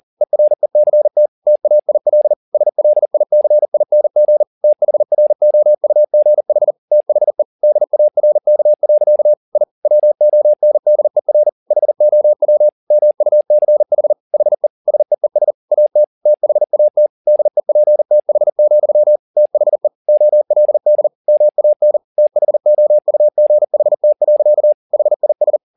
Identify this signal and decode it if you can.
Morse code, heard as a keyed tone, about 40 wpm, 600 Hz. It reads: ERECT TAIL SLICING THROUGH THE DARK. I WONDER HOW MUCH HE SEES AT THAT DEPTH, THE OLD MAN THOUGHT. HIS